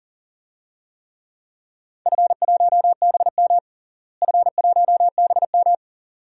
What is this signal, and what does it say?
Morse code, heard as a keyed tone, about 40 wpm, 700 Hz. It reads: F1BM F1BM
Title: DXCC-100 Call Signs - 4 Characters - Repeated 2x - 40wpm